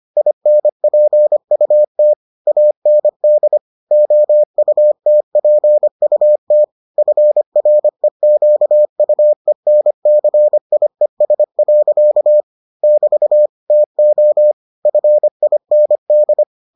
Morse code, heard as a keyed tone, about 25 wpm, 600 Hz. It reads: INPUT AND OUTPUT FREQUENCIES. <BT> TO FIND